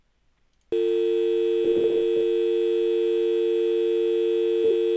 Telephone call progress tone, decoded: dial tone